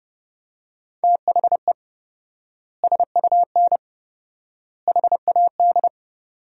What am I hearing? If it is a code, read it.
Morse code, heard as a keyed tone, about 30 wpm, 700 Hz. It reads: THE SUN HAD
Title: 4x Word-Spacing - Sentences from Top 300 Words - 30wpm